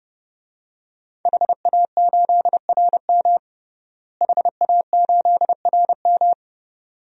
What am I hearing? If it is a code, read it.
Morse code, heard as a keyed tone, about 30 wpm, 700 Hz. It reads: HA8RM HA8RM
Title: DXCC-100 Call Signs - 5 Characters - Repeated 2x - 30wpm